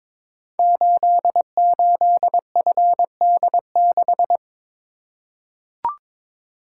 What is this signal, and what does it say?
Morse code, heard as a keyed tone, about 22 wpm, 700 Hz. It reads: 88FD6
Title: Code Groups - 5 Letter:Numbers - 22wpm